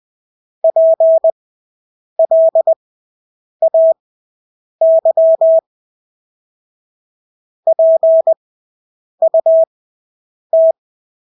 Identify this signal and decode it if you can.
Morse code, heard as a keyed tone, about 20 wpm, spaced as for 8 wpm, 650 Hz. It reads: PLAY PUT
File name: Engelske ord 20/8 wpm